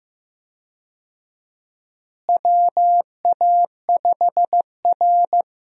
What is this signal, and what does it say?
Morse code, heard as a keyed tone, about 15 wpm, 700 Hz. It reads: WA5R
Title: USA Call Signs - 4 Characters - Repeated 2x - 15wpm